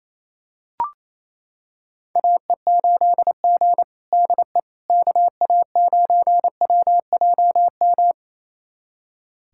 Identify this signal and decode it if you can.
Morse code, heard as a keyed tone, about 28 wpm, 700 Hz. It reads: AE8G DE KA9WJM
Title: USA Callsign DE Callsign 28wpm